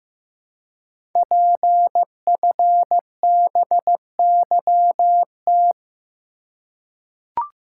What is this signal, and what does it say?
Morse code, heard as a keyed tone, about 15 wpm, 700 Hz. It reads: PFBYT
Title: Code Groups - 5 Letters - 15wpm